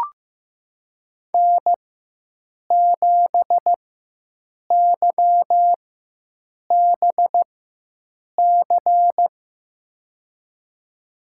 Morse code, heard as a keyed tone, about 15 wpm, spaced as for 7 wpm, 700 Hz. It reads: N7YBC
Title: USA Call Signs - 5 Characters - 4x Character Spacing - 15wpm